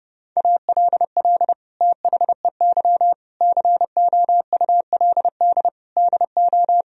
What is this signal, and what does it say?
Morse code, heard as a keyed tone, about 32 wpm, 700 Hz. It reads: ALL THEY COULD DO
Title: Sentences from Top 1000 Words - Review Part 1 - 30wpm